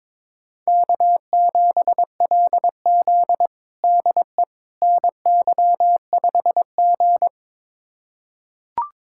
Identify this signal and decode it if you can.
Morse code, heard as a keyed tone, about 21 wpm, 700 Hz. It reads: K7LZ DE NY5G
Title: USA Callsign DE Callsign - 4 Characters - 22wpm